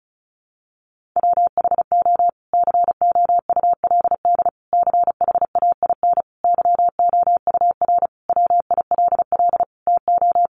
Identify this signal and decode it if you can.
Morse code, heard as a keyed tone, about 34 wpm, 700 Hz. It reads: WHO COULD CHAIN YOUR WILL TO
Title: Sentences from Top 1500 Words 35wpm